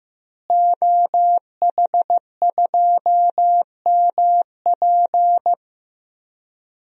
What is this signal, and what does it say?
Morse code, heard as a keyed tone, about 15 wpm, 700 Hz. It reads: OH2MP
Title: DXCC-100 Call Signs - 5 Characters - Repeated 2x - 15wpm